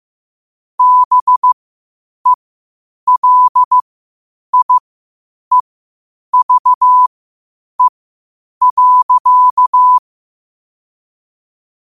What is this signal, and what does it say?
Morse code, heard as a keyed tone, about 15 wpm, spaced as for 8 wpm, 1000 Hz. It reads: BELIEVE.
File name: qotdfiles / 8wpm